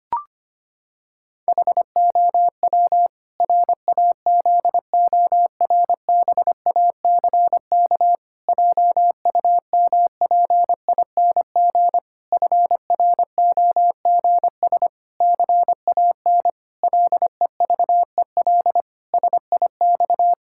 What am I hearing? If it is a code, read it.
Morse code, heard as a keyed tone, about 25 wpm, 700 Hz. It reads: HOW RAZORBACK JUMPING FROGS CAN LEVEL SIX
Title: Pangram Challenge v2 - 25wpm